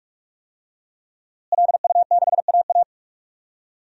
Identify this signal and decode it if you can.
Morse code, heard as a keyed tone, about 46 wpm, 700 Hz. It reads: LU6AA
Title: DXCC-100 Call Signs - 5 Characters - Repeated 3x - 45wpm